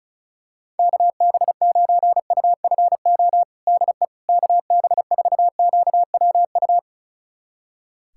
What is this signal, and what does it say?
Morse code, heard as a keyed tone, about 36 wpm, 700 Hz. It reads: KB9UFO DE KB4QWU
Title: USA Callsign DE Callsign - 6 Characters - 35wpm